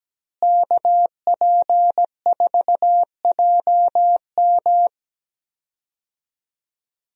Morse code, heard as a keyed tone, about 17 wpm, 700 Hz. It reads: KP4JM
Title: DXCC-100 Call Signs - 5 Characters - Repeated 3x - 17wpm